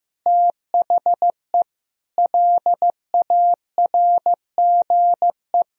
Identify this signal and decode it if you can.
Morse code, heard as a keyed tone, about 15 wpm, 700 Hz. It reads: THE LARGE
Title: Sentences from Top 1000 Words - Review Part 4 - 15wpm